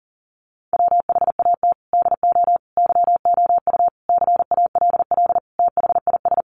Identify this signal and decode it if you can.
Morse code, heard as a keyed tone, about 40 wpm, 700 Hz. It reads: WHAT DO YOU CALL THIS
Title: Sentences from Top 1400 Words 40wpm